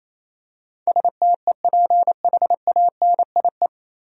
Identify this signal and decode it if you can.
Morse code, heard as a keyed tone, about 28 wpm, 700 Hz. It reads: STEPHANIE
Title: USA Common Names - 28wpm